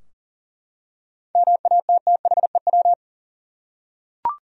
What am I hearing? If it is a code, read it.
Morse code, heard as a keyed tone, about 39 wpm, 700 Hz. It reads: MATTHEW
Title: USA Common Names - 40wpm